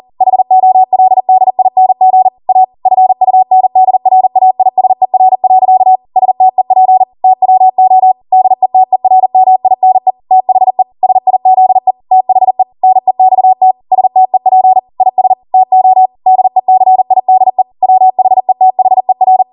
Morse code, heard as a keyed tone, about 40 wpm, 750 Hz. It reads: HOLDING A FUNDRAISER. STEP TWO DETERMINE THE SIZE THE NEXT STEP IS TO DECIDE WHETHER